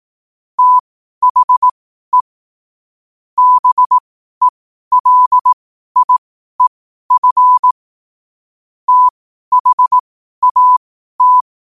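Morse code, heard as a keyed tone, about 18 wpm, spaced as for 12 wpm, 1000 Hz. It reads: THE BELIEF THAT
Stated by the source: Morse code at 12 wpm.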